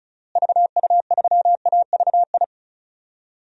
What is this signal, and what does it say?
Morse code, heard as a keyed tone, about 34 wpm, 700 Hz. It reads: VU3AVI
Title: DXCC-100 Call Signs - 6 Characters - Repeated 2x - 35wpm